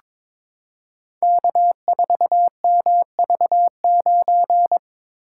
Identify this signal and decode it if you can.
Morse code, heard as a keyed tone, about 22 wpm, 700 Hz. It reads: K4MV9
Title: Code Groups - 5 Letter:Numbers - 22wpm